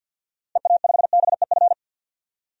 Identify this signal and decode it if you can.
Morse code, heard as a keyed tone, about 52 wpm, 700 Hz. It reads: EA5BEF
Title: DXCC-100 Call Signs - 6 Characters - Repeated 2x - 50wpm